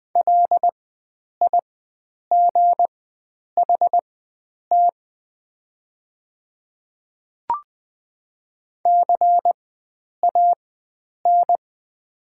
Morse code, heard as a keyed tone, about 20 wpm, spaced as for 9 wpm, 700 Hz. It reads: LIGHT CAN
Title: 4x Character Spacing - Top 300 Words - 20wpm